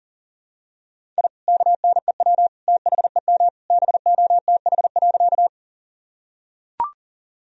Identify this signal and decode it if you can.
Morse code, heard as a keyed tone, about 40 wpm, 700 Hz. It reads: I KNEW THEM BOTH.
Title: Sentences from Top 1000 Words - Review Part 1 - 40wpm